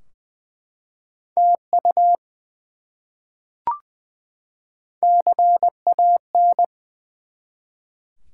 Morse code, heard as a keyed tone, about 20 wpm, 700 Hz. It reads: TU CAN
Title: Top 90 QSO Elements 20wpm